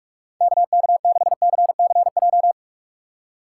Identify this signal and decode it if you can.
Morse code, heard as a keyed tone, about 43 wpm, 700 Hz. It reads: KK6CCJ
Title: USA Call Signs - 2x Character Spacing - 45wpm